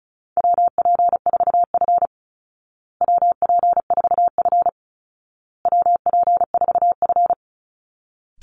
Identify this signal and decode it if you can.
Morse code, heard as a keyed tone, about 34 wpm, 700 Hz. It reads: WP4F WP4F WP4F
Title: Call Signs Repeated 2x - 35wpm